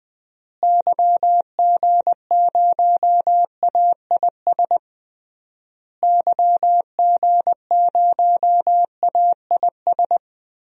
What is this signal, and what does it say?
Morse code, heard as a keyed tone, about 21 wpm, 700 Hz. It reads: YG0AIS YG0AIS
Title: DXCC-100 Call Signs - 6 Characters - Repeated 3x - 20wpm